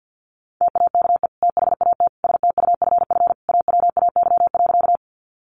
Morse code, heard as a keyed tone, about 50 wpm, 700 Hz. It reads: TAKE THAT STUFF AWAY.
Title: Sentences from Top 1400 Words 50wpm